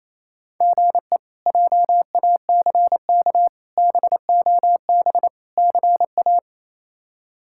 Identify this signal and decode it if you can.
Morse code, heard as a keyed tone, about 28 wpm, 700 Hz. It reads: GE JACK BOB CA
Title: K1USN SST QSO Practice - 28wpm